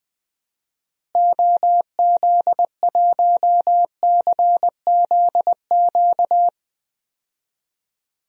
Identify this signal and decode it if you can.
Morse code, heard as a keyed tone, about 20 wpm, 700 Hz. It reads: OZ1CZQ
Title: DXCC-100 Call Signs - 6 Characters - Repeated 2x - 20wpm